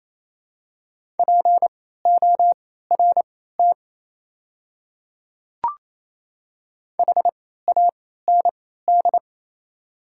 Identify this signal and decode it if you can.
Morse code, heard as a keyed tone, about 28 wpm, spaced as for 16 wpm, 700 Hz. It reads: PORT HAND
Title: 3x Character Spacing - Top 200 Words - 28wpm